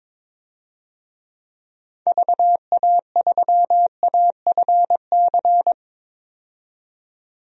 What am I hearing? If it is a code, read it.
Morse code, heard as a keyed tone, about 22 wpm, 700 Hz. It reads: VA3AFC
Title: DXCC-100 Call Signs - 6 Characters - 22wpm